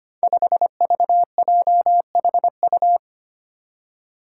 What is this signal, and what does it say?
Morse code, heard as a keyed tone, about 25 wpm, 700 Hz. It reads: 5VJHU
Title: Five Challenging Letter-Number Combinations 25wpm